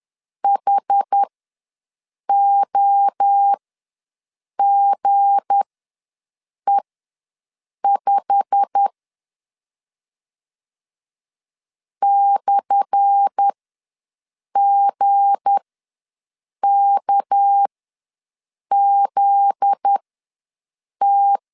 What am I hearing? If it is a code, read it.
Morse code, heard as a keyed tone, about 10 wpm, spaced as for 6 wpm, 800 Hz. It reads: HOGE5 /GKZT
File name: LEZIONI / 10 SIGNS / 10A SLASH ws 10wpm 4121